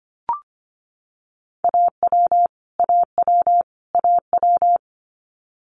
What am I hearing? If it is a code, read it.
Morse code, heard as a keyed tone, about 25 wpm, 700 Hz. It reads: AW AW AW
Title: Multi-letter Phonemes Repeated 3x 25wpm